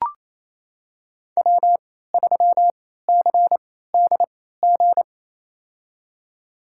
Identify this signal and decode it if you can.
Morse code, heard as a keyed tone, about 28 wpm, spaced as for 16 wpm, 700 Hz. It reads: W3CDG
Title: USA Call Signs - 3x Character Spacing - 28wpm